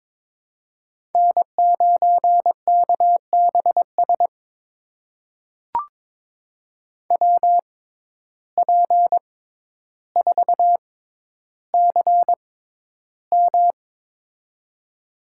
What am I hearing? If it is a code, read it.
Morse code, heard as a keyed tone, about 21 wpm, spaced as for 11 wpm, 700 Hz. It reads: N9KBS WP4CM
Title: USA Call Signs - 5 Characters - 6x Character Spacing - 22wpm